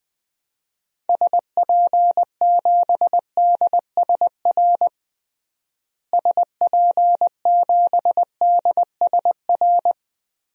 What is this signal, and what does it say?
Morse code, heard as a keyed tone, about 20 wpm, 700 Hz. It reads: SP7DSR SP7DSR
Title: DXCC-100 Call Signs - 6 Characters - Repeated 3x - 20wpm